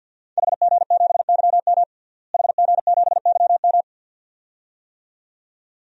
Morse code, heard as a keyed tone, about 51 wpm, 700 Hz. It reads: HG7YM HG7YM
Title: DXCC-100 Call Signs - 5 Characters - Repeated 2x - 50wpm